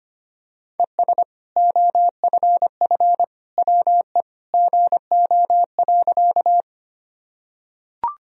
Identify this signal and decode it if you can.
Morse code, heard as a keyed tone, about 25 wpm, 700 Hz. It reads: ES OFF WE GO.
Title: Sentences from Top 200 Words and Abbreviations 25wpm